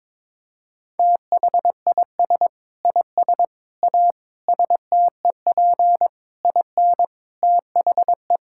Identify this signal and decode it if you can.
Morse code, heard as a keyed tone, about 22 wpm, 700 Hz. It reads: THIS IS A STEP IN THE
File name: practice / 22wpm / Sentences with 8-Word Phrases - 22wpm